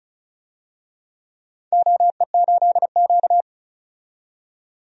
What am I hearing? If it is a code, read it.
Morse code, heard as a keyed tone, about 36 wpm, 700 Hz. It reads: OE8Q
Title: DXCC-100 Call Signs - 4 Characters - 35wpm